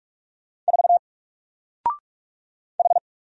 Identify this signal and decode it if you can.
Morse code, heard as a keyed tone, about 44 wpm, 700 Hz. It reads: 4 H
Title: Single Letter-Number 45wpm